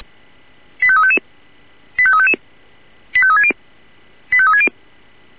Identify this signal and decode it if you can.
Five-tone selective call (ZVEI), heard as quick runs of five tones, tones of about 70 ms. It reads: 85355 85355 85360 85360
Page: /496/SDR/ZVEI/85360/